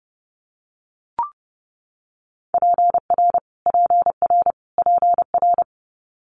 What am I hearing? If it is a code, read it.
Morse code, heard as a keyed tone, about 30 wpm, 700 Hz. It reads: PR PR PR
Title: Multi-letter Phonemes Repeated 3x 30wpm